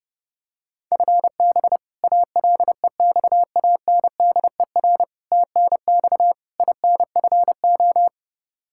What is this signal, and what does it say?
Morse code, heard as a keyed tone, about 30 wpm, 700 Hz. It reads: FB ALEXANDER TNX INFO
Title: K1USN SST QSO Practice 30wpm